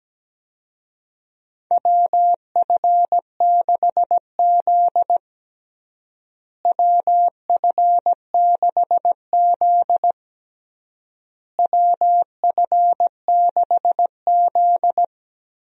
Morse code, heard as a keyed tone, about 17 wpm, 700 Hz. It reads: WF6Z WF6Z WF6Z
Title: USA Call Signs - 4 Characters - Repeated 3x - 17wpm